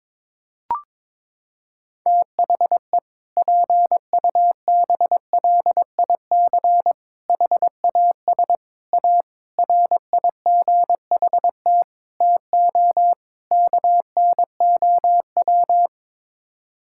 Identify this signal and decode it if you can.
Morse code, heard as a keyed tone, about 22 wpm, 700 Hz. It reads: THE PUBLIC HAS A RIGHT TO KNOW
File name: practice / 22wpm / Sets of 7 Words - 22wpm